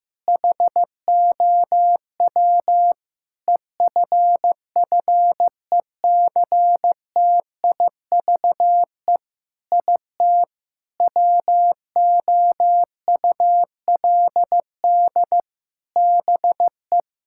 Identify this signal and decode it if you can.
Morse code, heard as a keyed tone, about 16 wpm, 700 Hz. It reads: HOW EFFECTIVE IT WOULD BE
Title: Sentences from Top 1000 Words - Review Part 6 - 15wpm